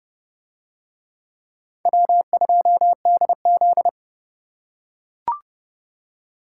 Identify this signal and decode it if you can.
Morse code, heard as a keyed tone, about 27 wpm, 700 Hz. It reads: W2DZ